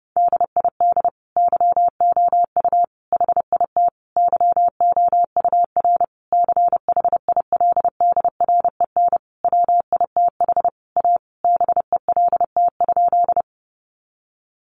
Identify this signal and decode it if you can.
Morse code, heard as a keyed tone, about 30 wpm, 700 Hz. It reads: DID YOU HIT YOUR CHILDREN WITH A BELT?
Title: Sentences from Top 1400 Words 30wpm